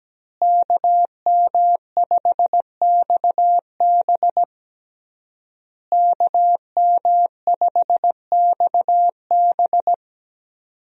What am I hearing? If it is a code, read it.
Morse code, heard as a keyed tone, about 17 wpm, 700 Hz. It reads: KM5XB KM5XB
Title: USA Call Signs - 5 Characters - Repeated 3x - 17wpm